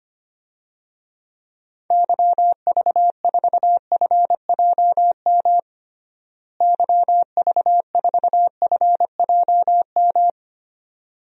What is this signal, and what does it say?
Morse code, heard as a keyed tone, about 25 wpm, 700 Hz. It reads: YV4FJM YV4FJM
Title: DXCC-100 Call Signs - 6 Characters - Repeated 2x - 25wpm